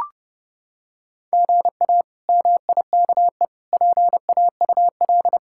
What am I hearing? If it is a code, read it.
Morse code, heard as a keyed tone, about 30 wpm, 700 Hz. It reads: GA MIKE PAUL
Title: K1USN SST QSO Practice - 30wpm